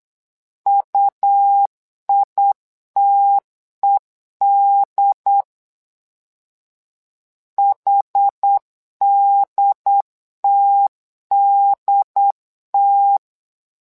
Morse code, heard as a keyed tone, about 8 wpm, 800 Hz. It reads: UITED HDTDT
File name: LEZIONI / 2 UHDN / 2 UHD 8wpm 2258